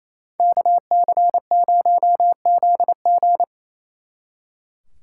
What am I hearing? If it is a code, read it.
Morse code, heard as a keyed tone, about 33 wpm, 700 Hz. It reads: KC0ZG
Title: Speed Racing - USA 2x2 Call Signs 28wpm